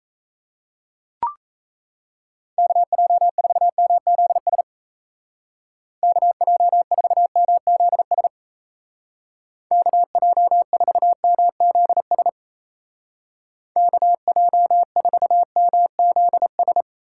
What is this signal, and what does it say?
Morse code, heard as a keyed tone, about 34 wpm, 700 Hz. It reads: KJ4MZS KJ4MZS KJ4MZS KJ4MZS
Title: Speed Racing - USA Call Signs 28wpm